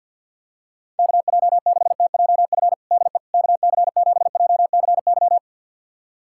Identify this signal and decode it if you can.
Morse code, heard as a keyed tone, about 47 wpm, 700 Hz. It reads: KJ6TJF DE KC7JCY
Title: USA Callsign DE Callsign - 6 Characters - 50wpm